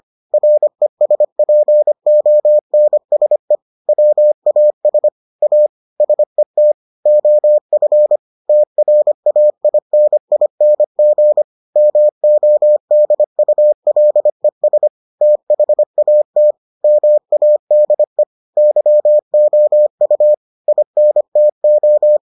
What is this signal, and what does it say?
Morse code, heard as a keyed tone, about 25 wpm, 600 Hz. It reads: RESPONSE WAS A SET OF TRAINING MODULES THAT MADE YOU INTO